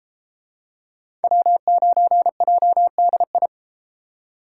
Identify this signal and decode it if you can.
Morse code, heard as a keyed tone, about 31 wpm, 700 Hz. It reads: W9JDI